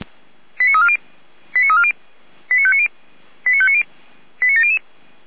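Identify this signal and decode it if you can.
Five-tone selective call (ZVEI), heard as quick runs of five tones, tones of about 70 ms. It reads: 89340 89340 89590 89590 89700
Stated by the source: ZVEI five-tone selective call 89230 received by SDR, fire brigade Orlamünde (feuerwehr-orlamuende).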